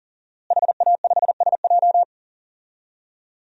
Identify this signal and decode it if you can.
Morse code, heard as a keyed tone, about 40 wpm, 700 Hz. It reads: HA5SJ
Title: DXCC-100 Call Signs - 5 Characters - Repeated 3x - 40wpm